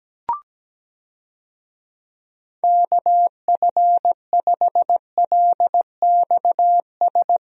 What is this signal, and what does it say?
Morse code, heard as a keyed tone, about 17 wpm, 700 Hz. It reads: KF5LXS
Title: USA Call Signs - 6 Characters - Repeated 2x - 17wpm